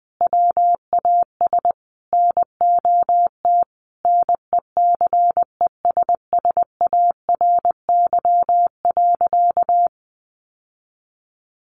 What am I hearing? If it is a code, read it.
Morse code, heard as a keyed tone, about 20 wpm, 700 Hz. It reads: WAS NOT NECESSARY.
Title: Sentences from Top 800 Words 20wpm